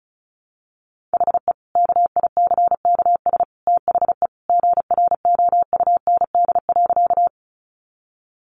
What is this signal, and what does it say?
Morse code, heard as a keyed tone, about 34 wpm, 700 Hz. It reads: HE KICKS THE GROUND.
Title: Additional Easy Phrases 35wpm